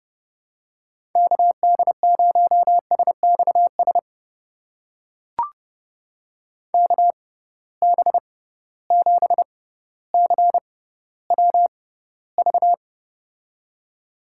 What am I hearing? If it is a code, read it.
Morse code, heard as a keyed tone, about 30 wpm, spaced as for 15 wpm, 700 Hz. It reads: KD0SXS KB7CWV
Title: USA Call Signs - 6 Characters - 6x Character Spacing - 30wpm